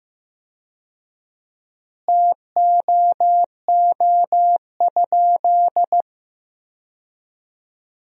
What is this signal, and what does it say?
Morse code, heard as a keyed tone, about 15 wpm, 700 Hz. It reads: TOO?
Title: 4x Word-Spacing - Sentences from Top 500 Words - 15wpm